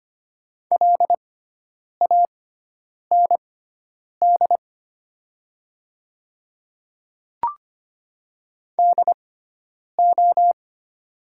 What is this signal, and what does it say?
Morse code, heard as a keyed tone, about 25 wpm, spaced as for 9 wpm, 700 Hz. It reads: LAND DO
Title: 6x Character Spacing - Top 200 Words - 25wpm